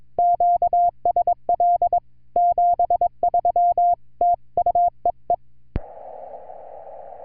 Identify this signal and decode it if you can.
Morse code, heard as a keyed tone, about 22 wpm, 700 Hz. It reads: QSL 73 TU EE
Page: N0FW 30JUL08 0117Z 80MTR CW 80MTRS WAS VERY GOOD ON 30JUL2008 TO USA